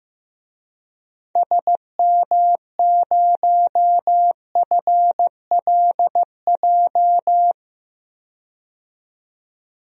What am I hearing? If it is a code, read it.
Morse code, heard as a keyed tone, about 15 wpm, 700 Hz. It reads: SM0FLJ
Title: DXCC-100 Call Signs - 6 Characters - 15wpm